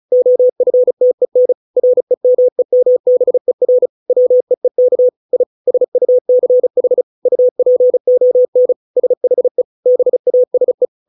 Morse code, heard as a keyed tone, about 34 wpm, 500 Hz. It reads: OFTEN REMEMBER WEEK I SUCH UPON SHE BASE